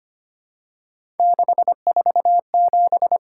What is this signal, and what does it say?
Morse code, heard as a keyed tone, about 25 wpm, 700 Hz. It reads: 647
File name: Three Letter-Number Combinations - 25wpm